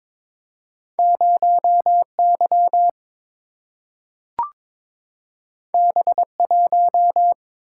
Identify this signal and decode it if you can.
Morse code, heard as a keyed tone, about 22 wpm, 700 Hz. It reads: 0 Y B 1
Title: Single-Letter Single-Number 22wpm